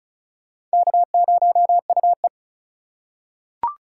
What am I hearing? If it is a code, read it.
Morse code, heard as a keyed tone, about 36 wpm, 700 Hz. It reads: K0UE
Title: USA Call Signs - 4 Characters - 3x Character Spacing - 35wpm